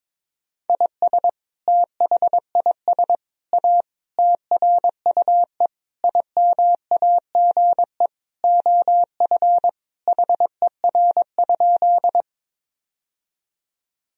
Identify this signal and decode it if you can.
Morse code, heard as a keyed tone, about 22 wpm, 700 Hz. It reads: IS THIS A TRUE IMAGE OF HER?
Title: Sentences from Top 1000 Words - Review Part 4 - 22wpm